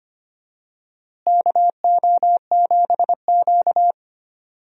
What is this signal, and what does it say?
Morse code, heard as a keyed tone, about 25 wpm, 700 Hz. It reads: KO7Q
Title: USA Call Signs - 4 Characters - Repeated 2x - 25wpm